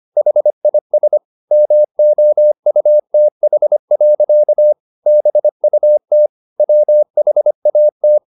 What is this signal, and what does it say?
Morse code, heard as a keyed tone, about 25 wpm, 600 Hz. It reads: HIS MOUTH. BUT WHAT
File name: the old man and the sea part 1 25wpm